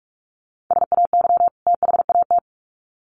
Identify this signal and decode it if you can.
Morse code, heard as a keyed tone, about 44 wpm, 700 Hz. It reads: SAY THAT
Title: Sets of 2 Words 45wpm